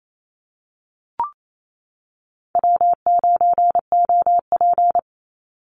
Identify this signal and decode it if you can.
Morse code, heard as a keyed tone, about 28 wpm, 700 Hz. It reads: W9OP
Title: Call Signs - 28wpm